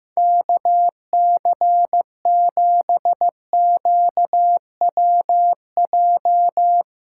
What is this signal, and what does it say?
Morse code, heard as a keyed tone, about 15 wpm, 700 Hz. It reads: KC7QWJ
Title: USA Callsign DE Callsign - 6 Characters - 15wpm